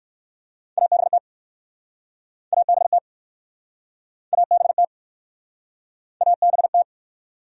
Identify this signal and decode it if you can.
Morse code, heard as a keyed tone, about 60 wpm, 700 Hz. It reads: ABT ABT ABT ABT